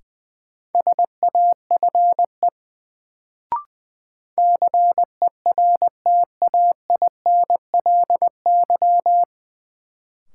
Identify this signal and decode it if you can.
Morse code, heard as a keyed tone, about 20 wpm, 700 Hz. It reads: SAFE CERTAINLY
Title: Sentences from Top 900 Words - New Word Drill - 20wpm